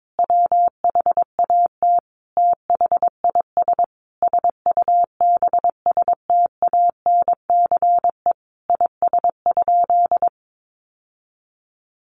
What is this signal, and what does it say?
Morse code, heard as a keyed tone, about 22 wpm, 700 Hz. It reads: WHAT THIS SUBSTANCE IS?
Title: Sentences from Top 1500 Words 22wpm